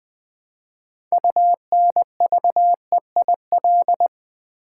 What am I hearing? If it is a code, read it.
Morse code, heard as a keyed tone, about 20 wpm, 700 Hz. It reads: UNVEIL
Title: 6 Letter Words - First 3 Letters Given - 20wpm